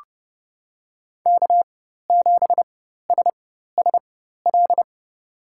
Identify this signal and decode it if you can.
Morse code, heard as a keyed tone, about 30 wpm, spaced as for 14 wpm, 700 Hz. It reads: K7SSL
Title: USA Call Signs - 4x Character Spacing - 30wpm